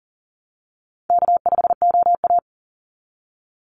Morse code, heard as a keyed tone, about 39 wpm, 700 Hz. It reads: K5OA
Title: Call Signs Repeated 2x - 40wpm